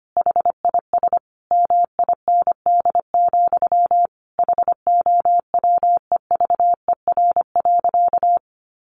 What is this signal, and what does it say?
Morse code, heard as a keyed tone, about 25 wpm, 700 Hz. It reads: HIS MIND, HOWEVER.
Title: Sentences from Top 1100 Words 25wpm